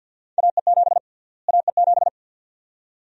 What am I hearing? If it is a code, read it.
Morse code, heard as a keyed tone, about 52 wpm, 700 Hz. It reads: AE7 AE7
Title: USA Call Sign Prefixes - Repeated 2x - 50wpm